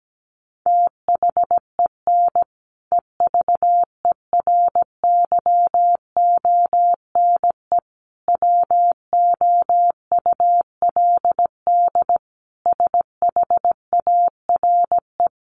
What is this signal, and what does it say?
Morse code, heard as a keyed tone, about 17 wpm, 700 Hz. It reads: THEN EVERYONE WOULD SHARE
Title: Sentences from Top 1400 Words 17wpm